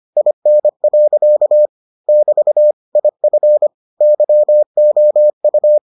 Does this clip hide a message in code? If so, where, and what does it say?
Yes: Morse code from 0.2 to 5.8 s, heard as a keyed tone, about 25 wpm, 600 Hz. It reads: IN. <BT> IF YOU